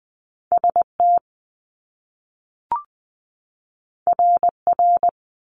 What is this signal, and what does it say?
Morse code, heard as a keyed tone, about 20 wpm, 700 Hz. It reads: ST RR
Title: Top Two Letter Combinations in Words 20wpm